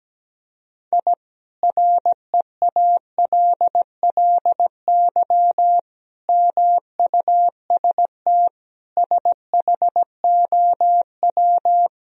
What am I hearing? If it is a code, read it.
Morse code, heard as a keyed tone, about 17 wpm, 700 Hz. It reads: I REALLY MUST SHOW
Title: Sentences from Top 1000 Words - Review Part 4 - 17wpm